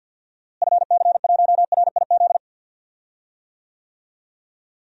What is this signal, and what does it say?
Morse code, heard as a keyed tone, about 46 wpm, 700 Hz. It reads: FC1RIZ
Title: DXCC-100 Call Signs - 6 Characters - Repeated 2x - 50wpm